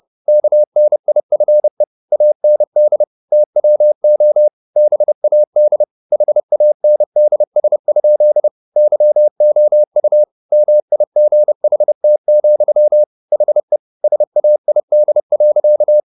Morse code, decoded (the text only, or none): KNIFE AND TWO BAD HANDS? YOU MIGHT, HE SAID.